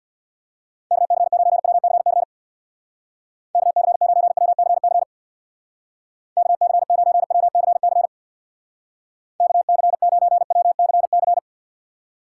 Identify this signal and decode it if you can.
Morse code, heard as a keyed tone, about 62 wpm, 700 Hz. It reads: KC9WCC KC9WCC KC9WCC KC9WCC
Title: Speed Racing - USA 2x3 Call Signs 50wpm